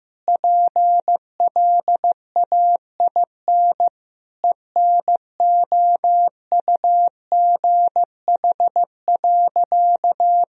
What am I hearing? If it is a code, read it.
Morse code, heard as a keyed tone, about 15 wpm, 700 Hz. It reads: PLAIN ENOUGH.
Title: Sentences from Top 1000 Words - Review Part 2 - 15wpm